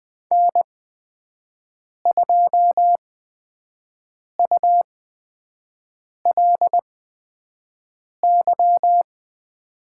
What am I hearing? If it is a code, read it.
Morse code, heard as a keyed tone, about 20 wpm, spaced as for 5 wpm, 700 Hz. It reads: N2ULY
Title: USA Call Signs - 5 Characters - 8x Character Spacing - 20wpm